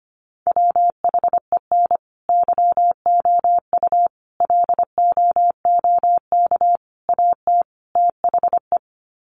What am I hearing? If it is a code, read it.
Morse code, heard as a keyed tone, about 25 wpm, 700 Hz. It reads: WHEN YOU LOOK AT THE
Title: Sets of 5 Words 25wpm